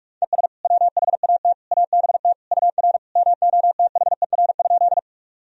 Morse code, heard as a keyed tone, about 46 wpm, 700 Hz. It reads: ES WHAT ABT UR MOTHER?
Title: Sentences from Top 200 Words and Abbreviations 45wpm